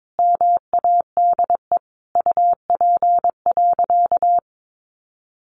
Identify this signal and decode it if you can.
Morse code, heard as a keyed tone, about 22 wpm, 700 Hz. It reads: MADE UP.
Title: Sentences from Top 700 Words 22wpm